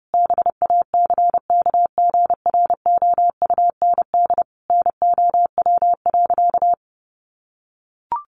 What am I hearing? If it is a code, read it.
Morse code, heard as a keyed tone, about 28 wpm, 700 Hz. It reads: BACKGROUND NOW.
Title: Sentences from Top 1300 Words 30wpm